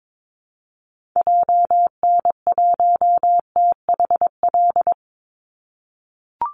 Morse code, heard as a keyed tone, about 22 wpm, 700 Hz. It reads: JN1THL
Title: Call Signs - 22wpm